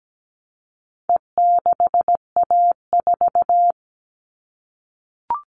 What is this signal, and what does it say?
Morse code, heard as a keyed tone, about 17 wpm, 700 Hz. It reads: E6A4
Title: All Cut Numbers - 4-Digit 17wpm